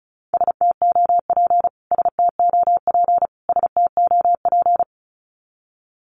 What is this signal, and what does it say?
Morse code, heard as a keyed tone, about 34 wpm, 700 Hz. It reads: STOP STOP STOP
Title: Top 500 Words Repeated 35wpm